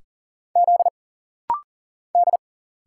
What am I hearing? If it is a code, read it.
Morse code, heard as a keyed tone, about 40 wpm, 700 Hz. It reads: Z D
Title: Single Letters - Rapid-Fire 40wpm